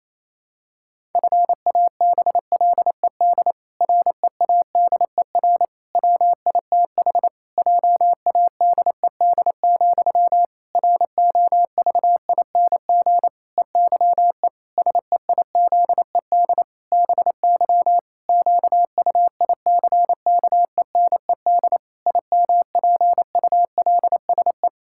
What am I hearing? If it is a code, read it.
Morse code, heard as a keyed tone, about 27 wpm, 700 Hz. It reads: FABLED READER WITH JADED, ROVING EYE SEIZED BY QUICKENED IMPULSE
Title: Pangram Challenge v2 - 28wpm